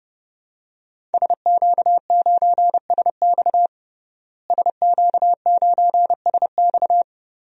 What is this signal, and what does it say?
Morse code, heard as a keyed tone, about 30 wpm, 700 Hz. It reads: SQ9SX SQ9SX
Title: DXCC-100 Call Signs - 5 Characters - Repeated 3x - 30wpm